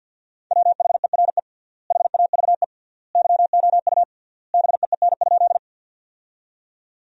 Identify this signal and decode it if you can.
Morse code, heard as a keyed tone, about 50 wpm, 700 Hz. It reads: WHERE HAVE YOU BEEN?
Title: 3x Word-Spacing - Sentences from Top 200 Words - 50wpm